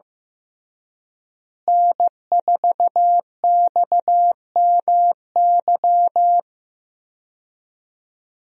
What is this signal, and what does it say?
Morse code, heard as a keyed tone, about 15 wpm, 700 Hz. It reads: N4XMY
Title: USA Call Signs - 5 Characters - Repeated 3x - 15wpm